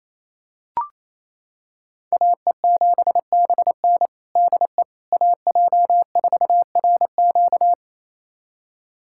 Morse code, heard as a keyed tone, about 28 wpm, 700 Hz. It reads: AE7BN DE AJ4RQ
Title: USA Callsign DE Callsign - 5 Characters - 28wpm